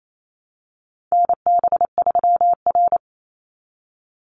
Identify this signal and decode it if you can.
Morse code, heard as a keyed tone, about 28 wpm, 700 Hz. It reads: NB3R
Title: Call Signs Repeated 2x - 28wpm